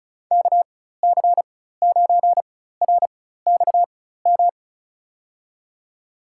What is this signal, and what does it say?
Morse code, heard as a keyed tone, about 36 wpm, spaced as for 16 wpm, 700 Hz. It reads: KC9RXM
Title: USA Call Signs - 6 Characters - 4x Character Spacing - 35wpm